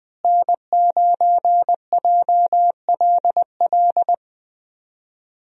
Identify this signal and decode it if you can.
Morse code, heard as a keyed tone, about 20 wpm, 700 Hz. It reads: N9JLL
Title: USA Call Signs - 5 Characters - Repeated 3x - 20wpm